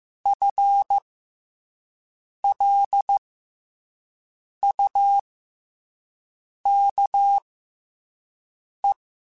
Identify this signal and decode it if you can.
Morse code, heard as a keyed tone, about 15 wpm, spaced as for 5 wpm, 800 Hz. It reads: FLUKE
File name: FL Runs / FL Runs mp3s / FL Runs @5